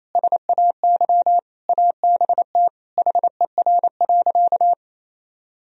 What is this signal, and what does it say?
Morse code, heard as a keyed tone, about 28 wpm, 700 Hz. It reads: SAY ABT HER.
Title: Sentences from Top 200 Words and Abbreviations 28wpm